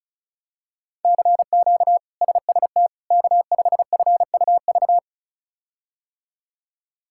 Morse code, heard as a keyed tone, about 36 wpm, 700 Hz. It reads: CQ SST K5FUV
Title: K1USN SST QSO Practice - 35wpm